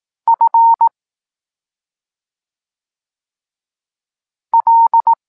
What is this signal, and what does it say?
Morse code, heard as a keyed tone, about 18 wpm, 920 Hz. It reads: F L